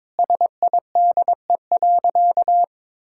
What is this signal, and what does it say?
Morse code, heard as a keyed tone, about 22 wpm, 700 Hz. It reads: SIDE.
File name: practice / 22wpm / Sentences with 2-Word Phrases - 22wpm